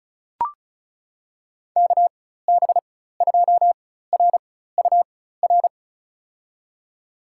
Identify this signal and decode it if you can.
Morse code, heard as a keyed tone, about 36 wpm, spaced as for 16 wpm, 700 Hz. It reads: KB2RUR
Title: USA Call Signs - 6 Characters - 4x Character Spacing - 35wpm